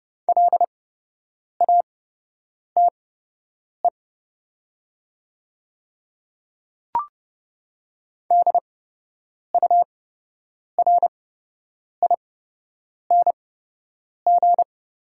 Morse code, heard as a keyed tone, about 30 wpm, spaced as for 8 wpm, 700 Hz. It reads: LATE DURING
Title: 8x Character Spacing - Top 500 Words - 30wpm